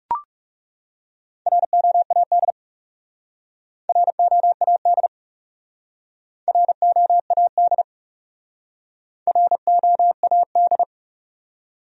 Morse code, heard as a keyed tone, about 36 wpm, 700 Hz. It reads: ROAD ROAD ROAD ROAD
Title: Speed-Racing Top 500 Words 30wpm